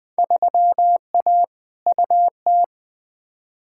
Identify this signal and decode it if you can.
Morse code, heard as a keyed tone, about 20 wpm, 700 Hz. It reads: 3A UT
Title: Field Day Exchanges - 20wpm